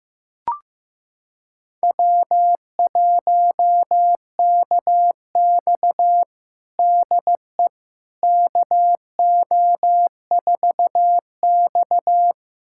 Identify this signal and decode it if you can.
Morse code, heard as a keyed tone, about 15 wpm, 700 Hz. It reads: W1KX DE KO4X
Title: USA Callsign DE Callsign - 4 Characters - 15wpm